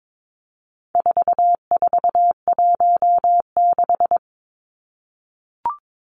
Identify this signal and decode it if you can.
Morse code, heard as a keyed tone, about 22 wpm, 700 Hz. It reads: 4416
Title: Numbers - 4-Digit 22wpm